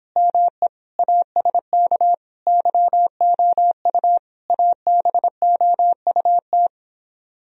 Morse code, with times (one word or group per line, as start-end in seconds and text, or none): 0.2-0.7 ME
1.0-2.1 ASK
2.5-4.2 YOU
4.5-6.7 ABOUT